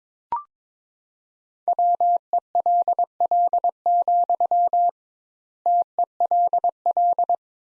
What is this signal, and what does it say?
Morse code, heard as a keyed tone, about 22 wpm, 700 Hz. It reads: WELL, TELL
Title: Sentences Top 200 Words - 22wpm